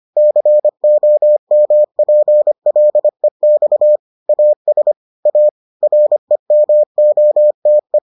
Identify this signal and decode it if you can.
Morse code, heard as a keyed tone, about 25 wpm, 600 Hz. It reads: COMPLEX AS A REMOTE